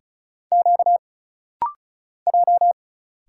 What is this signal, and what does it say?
Morse code, heard as a keyed tone, about 35 wpm, 700 Hz. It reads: Q J
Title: Single Letter-Number - Rapid-Fire 35wpm